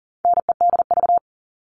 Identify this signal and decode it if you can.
Morse code, heard as a keyed tone, about 40 wpm, 700 Hz. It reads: NEDV
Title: All Cut Numbers - 4-Digit 40wpm